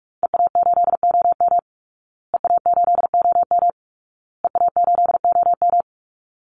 Morse code, heard as a keyed tone, about 44 wpm, 700 Hz. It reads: EA8OM EA8OM EA8OM
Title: Call Signs Repeated 2x - 45wpm